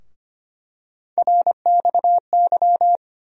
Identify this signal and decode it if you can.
Morse code, heard as a keyed tone, about 25 wpm, 700 Hz. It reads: RXY
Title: USA Call Sign Postfixes - 3 Letters 25wpm